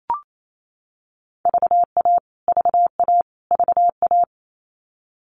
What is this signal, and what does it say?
Morse code, heard as a keyed tone, about 28 wpm, 700 Hz. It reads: VA VA VA
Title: US States Repeated 3x 28wpm